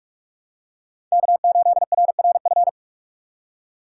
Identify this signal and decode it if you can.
Morse code, heard as a keyed tone, about 46 wpm, 700 Hz. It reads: K8RRF
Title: USA Call Signs - 5 Characters - 3x Character Spacing - 45wpm